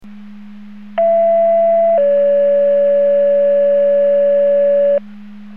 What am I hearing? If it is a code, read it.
Two-tone page, heard as a one-second tone, then a three-second tone, about 670 then 570 Hz. It is heard from 1.0 to 5.0 s.